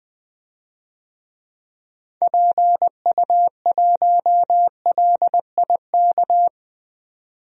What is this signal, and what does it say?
Morse code, heard as a keyed tone, about 20 wpm, 700 Hz. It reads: PU1LIK
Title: DXCC-100 Call Signs - 6 Characters - 20wpm